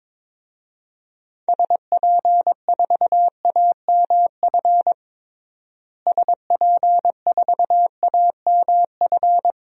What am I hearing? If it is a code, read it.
Morse code, heard as a keyed tone, about 22 wpm, 700 Hz. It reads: SP4AMF SP4AMF
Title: DXCC-100 Call Signs - 6 Characters - Repeated 2x - 22wpm